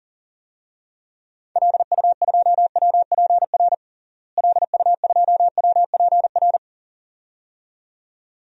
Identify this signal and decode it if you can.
Morse code, heard as a keyed tone, about 37 wpm, 700 Hz. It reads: LU2WPR LU2WPR
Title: DXCC-100 Call Signs - 6 Characters - Repeated 2x - 40wpm